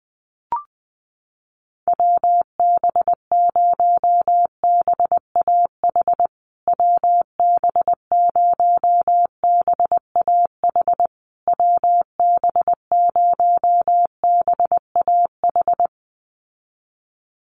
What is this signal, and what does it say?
Morse code, heard as a keyed tone, about 20 wpm, 700 Hz. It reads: WB0BAH WB0BAH WB0BAH
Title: USA Call Signs Repeated 3x 20wpm